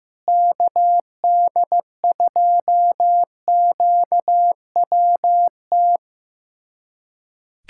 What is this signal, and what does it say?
Morse code, heard as a keyed tone, about 15 wpm, 700 Hz. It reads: KD2QWT
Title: USA Callsign DE Callsign - 6 Characters - 15wpm